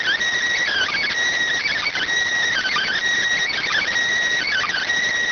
Emergency Alert System (SAME) tones (header bursts), heard as harsh buzzing bursts. It is heard from 0.6 to 4.8 s.